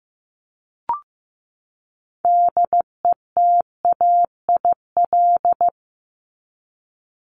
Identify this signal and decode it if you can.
Morse code, heard as a keyed tone, about 15 wpm, 700 Hz. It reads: DETAIL
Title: Top 1251-1500 Words 15wpm